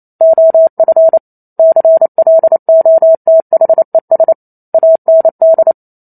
Morse code, heard as a keyed tone, about 28 wpm, 650 Hz. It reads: OF CLOTHES AND
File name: Chapter 2 - 28 WPM